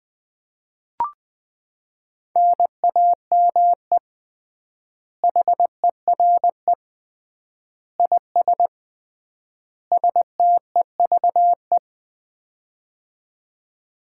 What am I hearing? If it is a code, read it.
Morse code, heard as a keyed tone, about 20 wpm, 700 Hz. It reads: NAME HERE IS STEVE
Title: Common QSO Phrases - 3x Word-Spacing - 20wpm